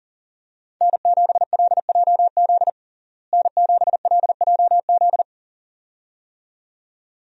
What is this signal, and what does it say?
Morse code, heard as a keyed tone, about 40 wpm, 700 Hz. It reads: N7LJZ N7LJZ
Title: USA Call Signs - 5 Characters - Repeated 2x - 40wpm